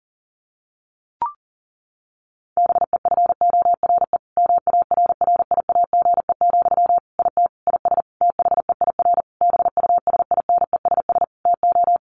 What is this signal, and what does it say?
Morse code, heard as a keyed tone, about 37 wpm, 700 Hz. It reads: BEFORE MARRIAGE, IT IS THEIR BUSINESS TO
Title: Sentences from Top 1500 Words 40wpm